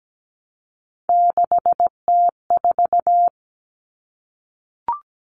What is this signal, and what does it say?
Morse code, heard as a keyed tone, about 17 wpm, 700 Hz. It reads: 6T4
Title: Common Cut Numbers - 3-Digit 17wpm